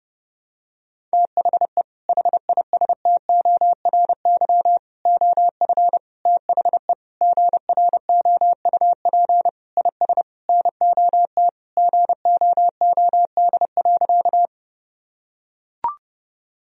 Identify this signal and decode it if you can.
Morse code, heard as a keyed tone, about 30 wpm, 700 Hz. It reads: THE HISTORY OF THE GROUP IS NOT GOOD.
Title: Sentences from Top 1000 Words - Review Part 1 - 30wpm